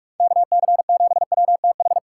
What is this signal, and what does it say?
Morse code, heard as a keyed tone, about 42 wpm, 700 Hz. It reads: KC7WTH
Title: USA Call Signs - 6 Characters - 4x Character Spacing - 45wpm